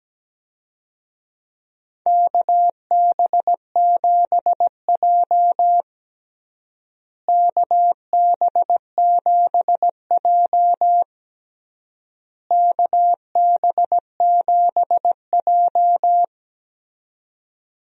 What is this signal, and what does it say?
Morse code, heard as a keyed tone, about 17 wpm, 700 Hz. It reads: KB7J KB7J KB7J
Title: USA Call Signs - 4 Characters - Repeated 3x - 17wpm